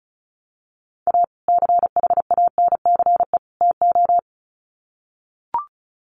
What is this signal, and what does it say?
Morse code, heard as a keyed tone, about 34 wpm, 700 Hz. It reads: A CHANCE TO
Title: Sets of 3 Words 35wpm